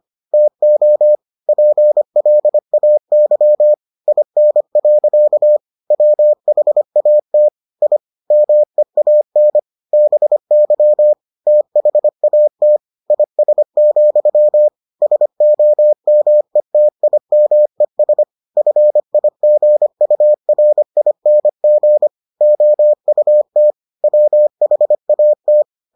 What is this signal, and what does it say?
Morse code, heard as a keyed tone, about 25 wpm, 600 Hz. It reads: TO PLAY IN. WHAT I MEAN BY THAT IS, SOMETIMES FIGURING OUT WHAT